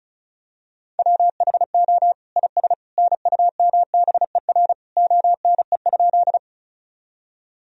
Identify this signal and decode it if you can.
Morse code, heard as a keyed tone, about 34 wpm, 700 Hz. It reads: WHO IS NUMBER ONE?
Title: Sentences from Top 1000 Words - Review Part 1 - 35wpm